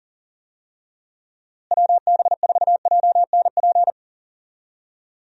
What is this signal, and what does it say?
Morse code, heard as a keyed tone, about 40 wpm, 700 Hz. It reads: WB4JNP
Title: USA Call Signs - 6 Characters - Repeated 3x - 40wpm